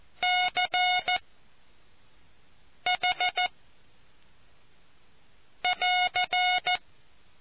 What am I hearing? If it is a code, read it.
Morse code, heard as a keyed tone, about 14 wpm, spaced as for 4 wpm, 730 Hz. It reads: CH <AR>